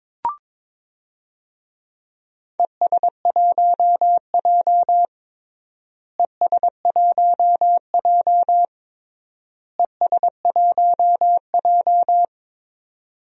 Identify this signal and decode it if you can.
Morse code, heard as a keyed tone, about 22 wpm, 700 Hz. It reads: ES1J ES1J ES1J
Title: DXCC-100 Call Signs - 4 Characters - Repeated 3x - 22wpm